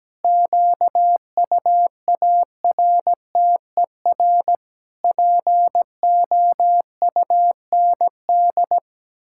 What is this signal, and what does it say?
Morse code, heard as a keyed tone, about 17 wpm, 700 Hz. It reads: QUARTER POUND
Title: Pangram Challenge v2 - 17wpm